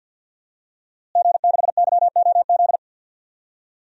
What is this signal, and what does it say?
Morse code, heard as a keyed tone, about 51 wpm, 700 Hz. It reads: G6YOZ
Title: DXCC-100 Call Signs - 5 Characters - Repeated 2x - 50wpm